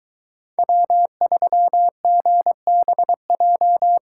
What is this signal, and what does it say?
Morse code, heard as a keyed tone, about 24 wpm, 700 Hz. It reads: W3GBJ